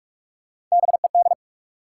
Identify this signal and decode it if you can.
Morse code, heard as a keyed tone, about 46 wpm, 700 Hz. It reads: BED
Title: Sentences from Top 500 Words - New Word Drill - 45wpm